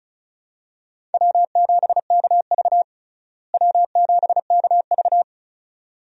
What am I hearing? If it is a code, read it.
Morse code, heard as a keyed tone, about 36 wpm, 700 Hz. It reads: W7KV W7KV
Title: USA Call Signs - 4 Characters - Repeated 2x - 35wpm